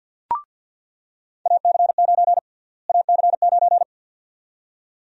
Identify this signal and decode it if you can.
Morse code, heard as a keyed tone, about 48 wpm, 700 Hz. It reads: AC9 AC9
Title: USA Call Sign Prefixes - Repeated 2x - 50wpm